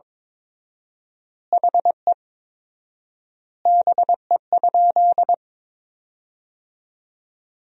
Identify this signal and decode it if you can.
Morse code, heard as a keyed tone, about 22 wpm, 700 Hz. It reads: HE BE?
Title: 4x Word-Spacing - Sentences from Top 200 Words - 22wpm